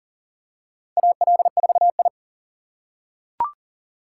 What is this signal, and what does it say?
Morse code, heard as a keyed tone, about 37 wpm, 700 Hz. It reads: AL4I
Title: USA Call Signs - 4 Characters - 3x Character Spacing - 40wpm